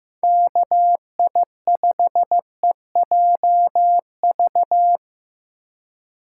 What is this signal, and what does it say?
Morse code, heard as a keyed tone, about 18 wpm, 700 Hz. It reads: KI5EJV
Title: Speed Racing - USA Call Signs 15wpm